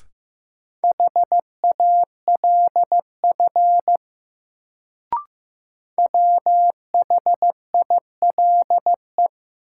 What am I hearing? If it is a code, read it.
Morse code, heard as a keyed tone, about 15 wpm, 700 Hz. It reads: HALF WHILE
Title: Sets of 3 Words - New Word Drill - 15wpm